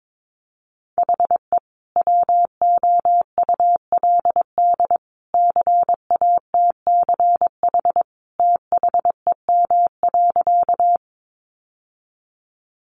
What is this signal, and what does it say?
Morse code, heard as a keyed tone, about 22 wpm, 700 Hz. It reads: HE WOULD CATCH THEM.
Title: Sentences from Top 1500 Words 22wpm